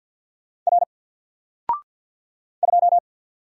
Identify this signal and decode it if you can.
Morse code, heard as a keyed tone, about 50 wpm, 700 Hz. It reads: R 2
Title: Single Letter-Number 50wpm